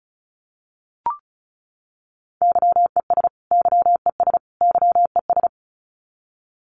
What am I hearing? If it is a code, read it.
Morse code, heard as a keyed tone, about 34 wpm, 700 Hz. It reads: YES YES YES
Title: Top 500 Words Repeated 35wpm